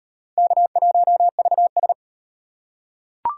Morse code, heard as a keyed tone, about 32 wpm, 700 Hz. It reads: K1VS